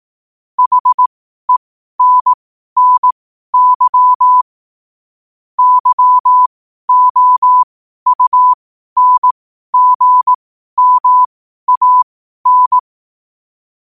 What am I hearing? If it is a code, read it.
Morse code, heard as a keyed tone, about 18 wpm, spaced as for 12 wpm, 1000 Hz. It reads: HENNY YOUNGMAN
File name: qotdfiles / 12wpm